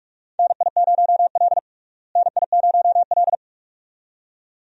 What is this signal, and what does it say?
Morse code, heard as a keyed tone, about 46 wpm, 700 Hz. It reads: NI0L NI0L
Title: USA Call Signs - 4 Characters - Repeated 2x - 45wpm